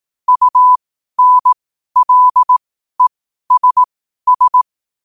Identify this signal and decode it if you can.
Morse code, heard as a keyed tone, about 18 wpm, spaced as for 12 wpm, 1000 Hz. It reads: UNLESS
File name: qotdfiles / 12wpm